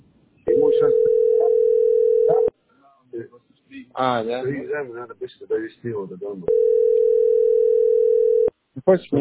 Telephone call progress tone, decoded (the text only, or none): ringback tone